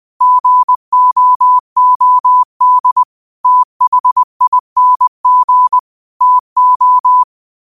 Morse code, heard as a keyed tone, about 20 wpm, 1000 Hz. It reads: GOOD THING TO
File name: qotdfiles / 20wpm